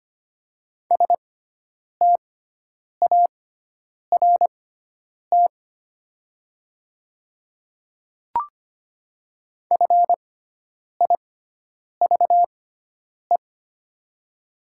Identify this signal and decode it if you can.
Morse code, heard as a keyed tone, about 25 wpm, spaced as for 9 wpm, 700 Hz. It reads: START FIVE
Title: 6x Character Spacing - Top 500 Words - 25wpm